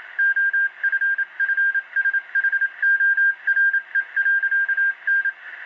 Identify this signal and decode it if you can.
Morse code, heard as a keyed tone, about 27 wpm, 1600 Hz. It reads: OFFSHORE.N